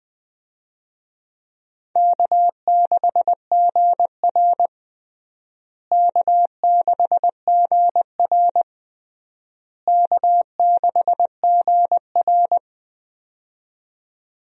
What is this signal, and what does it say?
Morse code, heard as a keyed tone, about 19 wpm, 700 Hz. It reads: K6GR K6GR K6GR
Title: USA Call Signs - 4 Characters - Repeated 3x - 20wpm